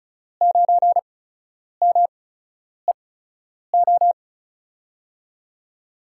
Morse code, heard as a keyed tone, about 35 wpm, spaced as for 10 wpm, 690 Hz. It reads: 9MEO
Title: USA Call Signs - 5 Characters - 8x Character Spacing - 35wpm